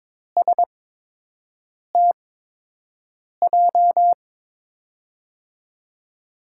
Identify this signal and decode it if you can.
Morse code, heard as a keyed tone, about 22 wpm, spaced as for 6 wpm, 700 Hz. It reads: STJ
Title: USA Call Signs - 6 Characters - 8x Character Spacing - 22wpm